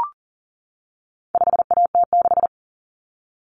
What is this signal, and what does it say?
Morse code, heard as a keyed tone, about 38 wpm, 700 Hz. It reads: 5AT6
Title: Common Cut Numbers - 4-Digit 40wpm